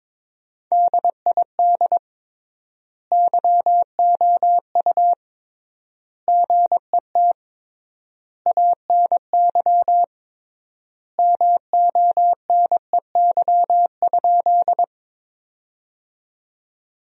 Morse code, heard as a keyed tone, about 22 wpm, 700 Hz. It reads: DID YOU GET ANY MONEY?
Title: 3x Word-Spacing - Sentences from Top 400 Words - 22wpm